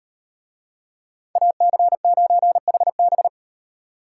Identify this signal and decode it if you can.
Morse code, heard as a keyed tone, about 38 wpm, 700 Hz. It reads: AC9HB